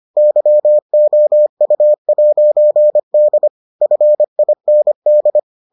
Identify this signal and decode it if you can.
Morse code, heard as a keyed tone, about 25 wpm, 600 Hz. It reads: YOU'D FIND